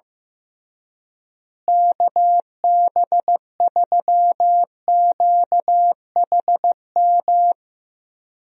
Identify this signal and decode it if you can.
Morse code, heard as a keyed tone, about 15 wpm, 700 Hz. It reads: KB3QHM
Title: USA Call Signs - 6 Characters - Repeated 3x - 15wpm